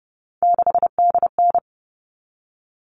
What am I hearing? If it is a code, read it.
Morse code, heard as a keyed tone, about 30 wpm, 700 Hz. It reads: 6DN
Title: All Cut Numbers - 3-Digit 30wpm